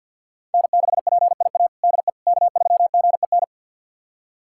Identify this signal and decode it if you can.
Morse code, heard as a keyed tone, about 46 wpm, 700 Hz. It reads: N6PIA DE K3GEN
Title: USA Callsign DE Callsign - 5 Characters - 50wpm